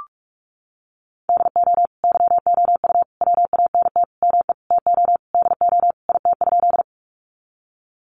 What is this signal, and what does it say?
Morse code, heard as a keyed tone, about 44 wpm, 700 Hz. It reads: DO YOU WANT ME TO DO IT?
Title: Sentences from Top 200 Words 45wpm